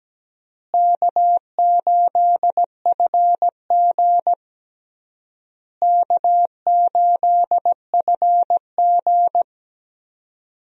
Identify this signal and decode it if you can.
Morse code, heard as a keyed tone, about 17 wpm, 700 Hz. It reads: K8FG K8FG
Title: USA Call Signs - 4 Characters - Repeated 3x - 17wpm